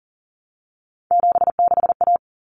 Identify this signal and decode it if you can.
Morse code, heard as a keyed tone, about 40 wpm, 700 Hz. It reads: 76A
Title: Common Cut Numbers - 3-Digit 40wpm